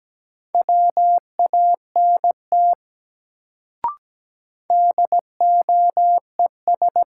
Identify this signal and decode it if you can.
Morse code, heard as a keyed tone, about 17 wpm, 700 Hz. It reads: WANT DOES
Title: Sentences from Top 200 Words - New Word Drill - 17wpm